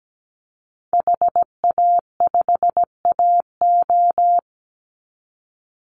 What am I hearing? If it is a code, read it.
Morse code, heard as a keyed tone, about 17 wpm, 700 Hz. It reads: HA5AO
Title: Call Signs - 17wpm